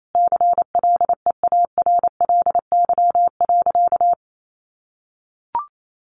Morse code, heard as a keyed tone, about 28 wpm, 700 Hz. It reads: CLEARLY.
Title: Sentences from Top 1300 Words 28wpm